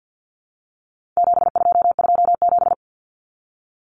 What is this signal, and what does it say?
Morse code, heard as a keyed tone, about 50 wpm, 700 Hz. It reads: 7227
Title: Common Cut Numbers - 4-Digit 50wpm